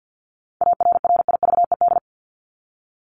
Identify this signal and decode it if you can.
Morse code, heard as a keyed tone, about 50 wpm, 700 Hz. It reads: ARRIVED
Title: ED Suffix - Top 100 Words with Base Words 50wpm